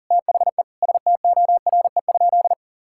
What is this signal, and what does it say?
Morse code, heard as a keyed tone, about 37 wpm, 700 Hz. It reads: THE STORE?
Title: Sentences from Top 1000 Words - Review Part 6 - 40wpm